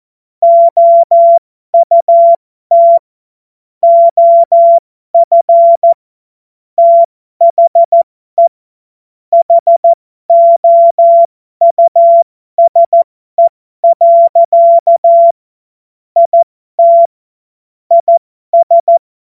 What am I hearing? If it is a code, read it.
Morse code, heard as a keyed tone, about 14 wpm, spaced as for 12 wpm, 680 Hz. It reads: OUT OF THE HOUSE. IT IS
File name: Morsekurs / Witze